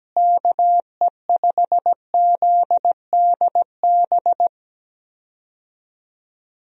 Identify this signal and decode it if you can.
Morse code, heard as a keyed tone, about 17 wpm, 700 Hz. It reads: KE5ZDB
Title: USA Call Signs - 6 Characters - Repeated 2x - 17wpm